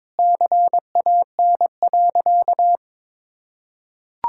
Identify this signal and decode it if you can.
Morse code, heard as a keyed tone, about 22 wpm, 700 Hz. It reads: CAN.
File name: practice / 22wpm / Sentences with 4-Word Phrases - 22wpm